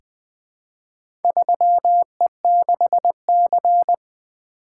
Morse code, heard as a keyed tone, about 20 wpm, 700 Hz. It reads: 3E6C
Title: Four Letter-Number Combinations - 20wpm